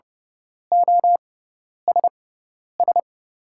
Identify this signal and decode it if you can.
Morse code, heard as a keyed tone, about 30 wpm, spaced as for 10 wpm, 700 Hz. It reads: OSS
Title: 6x Character Spacing - Top 300 Words - 30wpm